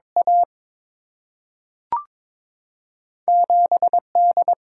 Morse code, heard as a keyed tone, about 22 wpm, 700 Hz. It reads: A 7 D
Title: Single-Letter Single-Number 22wpm